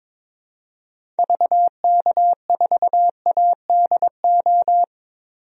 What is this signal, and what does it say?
Morse code, heard as a keyed tone, about 22 wpm, 700 Hz. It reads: VK4ADO
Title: DXCC-100 Call Signs - 6 Characters - 22wpm